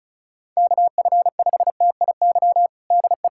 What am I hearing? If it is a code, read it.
Morse code, heard as a keyed tone, about 36 wpm, 700 Hz. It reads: KF5TIY DE
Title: USA Callsign DE Callsign - 6 Characters - 35wpm